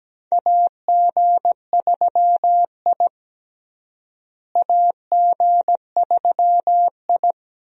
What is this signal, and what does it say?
Morse code, heard as a keyed tone, about 17 wpm, 700 Hz. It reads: AG3I AG3I
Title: USA Call Signs - 4 Characters - Repeated 2x - 17wpm